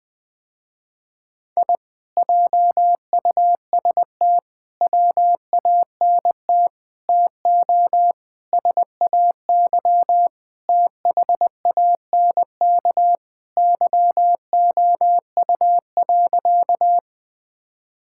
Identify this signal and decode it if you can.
Morse code, heard as a keyed tone, about 21 wpm, 700 Hz. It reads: I JUST WANT TO SAY THANK YOU.
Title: Sentences with 2-Word Phrases - Spoken Beginning - 20wpm